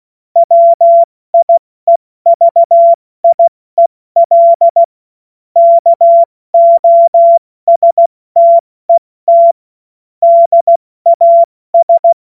Morse code, heard as a keyed tone, about 16 wpm, spaced as for 14 wpm, 680 Hz. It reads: WIEVIEL KOSTET DAS
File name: Morsekurs / Witze / Witz 04. Oktober 2022 70 F80